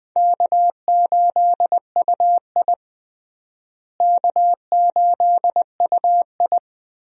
Morse code, heard as a keyed tone, about 19 wpm, 700 Hz. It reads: K8UI K8UI
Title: USA Call Signs - 4 Characters - Repeated 3x - 20wpm